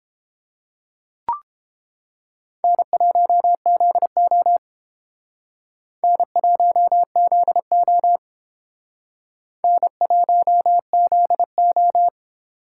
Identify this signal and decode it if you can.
Morse code, heard as a keyed tone, about 30 wpm, 700 Hz. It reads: N1ZO N1ZO N1ZO